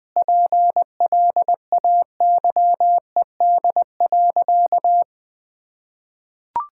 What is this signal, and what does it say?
Morse code, heard as a keyed tone, about 20 wpm, 700 Hz. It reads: PLAYED.
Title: Sentences with 3-Word Phrases - 20wpm